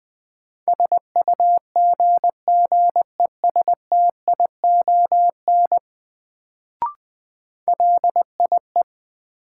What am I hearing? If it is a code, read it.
Morse code, heard as a keyed tone, about 21 wpm, 700 Hz. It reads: SUGGESTION LIE
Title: Sentences from Top 1600 Words - New Word Drill - 20wpm